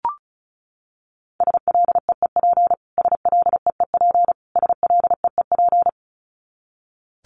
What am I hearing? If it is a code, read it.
Morse code, heard as a keyed tone, about 34 wpm, 700 Hz. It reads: SLEEP SLEEP SLEEP
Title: Top 500 Words Repeated 35wpm